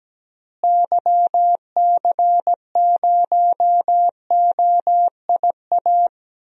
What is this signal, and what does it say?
Morse code, heard as a keyed tone, about 17 wpm, 700 Hz. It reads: YC0OIA
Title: DXCC-100 Call Signs - 6 Characters - Repeated 2x - 17wpm